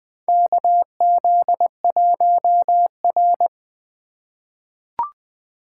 Morse code, heard as a keyed tone, about 21 wpm, 700 Hz. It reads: KZ1R
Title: USA Call Signs - 2x Character Spacing - 20wpm